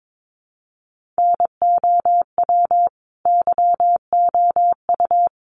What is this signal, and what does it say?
Morse code, heard as a keyed tone, about 22 wpm, 700 Hz. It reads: NOW YOU
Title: Sentences from Top 100 Words 22wpm